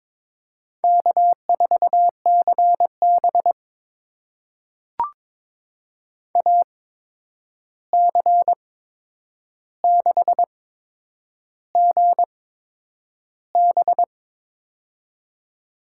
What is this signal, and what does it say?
Morse code, heard as a keyed tone, about 22 wpm, spaced as for 9 wpm, 700 Hz. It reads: K4CB AC6GB
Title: USA Call Signs - 8x Character Spacing - 22wpm